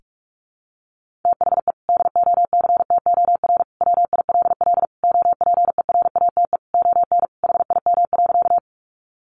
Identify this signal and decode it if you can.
Morse code, heard as a keyed tone, about 44 wpm, 700 Hz. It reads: THE DOCTOR WILL OPERATE ON HIM.
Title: Sentences from Top 1500 Words 45wpm